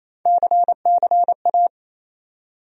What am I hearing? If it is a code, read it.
Morse code, heard as a keyed tone, about 28 wpm, 700 Hz. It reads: CCA
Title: USA Call Sign Postfixes - 3 Letters 28wpm